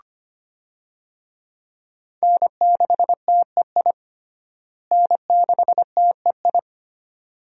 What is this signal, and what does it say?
Morse code, heard as a keyed tone, about 25 wpm, 700 Hz. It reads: N6TEI N6TEI
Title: USA Call Signs - 5 Characters - Repeated 2x - 25wpm